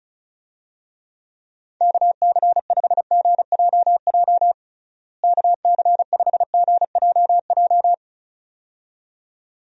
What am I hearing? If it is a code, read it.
Morse code, heard as a keyed tone, about 36 wpm, 700 Hz. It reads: KC5GJJ KC5GJJ
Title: USA Call Signs - 6 Characters - Repeated 2x - 35wpm